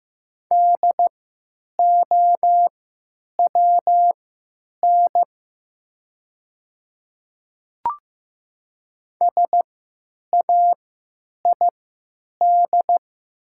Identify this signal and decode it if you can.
Morse code, heard as a keyed tone, about 15 wpm, spaced as for 9 wpm, 700 Hz. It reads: DOWN SAID
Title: 3x Character Spacing - Top 200 Words - 15wpm